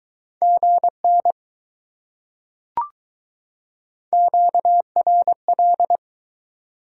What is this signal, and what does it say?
Morse code, heard as a keyed tone, about 20 wpm, 700 Hz. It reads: GN QRL